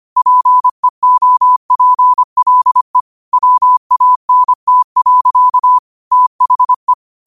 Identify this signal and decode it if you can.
Morse code, heard as a keyed tone, about 25 wpm, 1000 Hz. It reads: PEOPLE WANT. THE
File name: qotdfiles / 25wpm